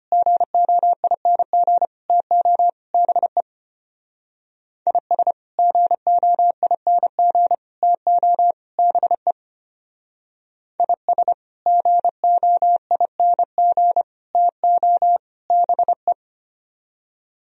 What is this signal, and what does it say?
Morse code, heard as a keyed tone, about 30 wpm, 700 Hz. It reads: GOING TO BE IS GOING TO BE IS GOING TO BE
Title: Speed-Racing - Sets of 4 Words 25wpm